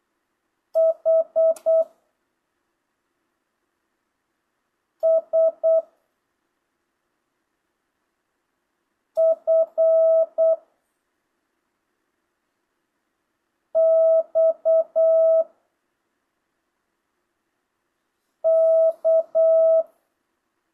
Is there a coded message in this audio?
Yes: Morse code from 0.8 to 19.8 s, heard as a keyed tone, about 8 wpm, 660 Hz. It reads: H S F X K